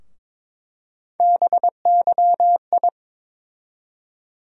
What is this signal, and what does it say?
Morse code, heard as a keyed tone, about 22 wpm, 700 Hz. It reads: BYI
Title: USA Call Sign Postfixes - 3 Letters 22wpm